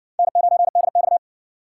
Morse code, heard as a keyed tone, about 60 wpm, 700 Hz. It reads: N9DX